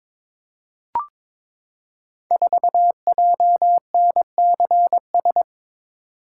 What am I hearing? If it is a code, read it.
Morse code, heard as a keyed tone, about 22 wpm, 700 Hz. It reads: 4JNCS
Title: Code Groups - 5 Letter:Numbers - 22wpm